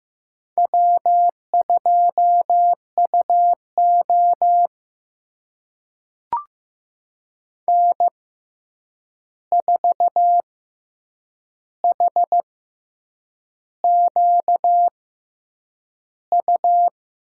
Morse code, heard as a keyed tone, about 15 wpm, spaced as for 7 wpm, 700 Hz. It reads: W2UO N4HQU
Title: USA Call Signs - 6x Character Spacing - 15wpm